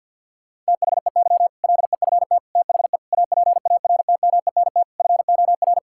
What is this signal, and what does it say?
Morse code, heard as a keyed tone, about 51 wpm, 700 Hz. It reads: THEY LEFT THE APARTMENT FOR
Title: Sentences from Top 1000 Words - Review Part 6 - 50wpm